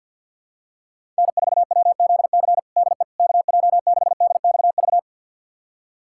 Morse code, heard as a keyed tone, about 52 wpm, 700 Hz. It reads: N4WZC DE KJ6DXV
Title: USA Callsign DE Callsign 50wpm